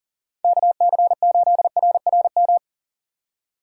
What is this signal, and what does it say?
Morse code, heard as a keyed tone, about 39 wpm, 700 Hz. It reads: KC8RRM
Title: USA Call Signs - 2x Character Spacing - 40wpm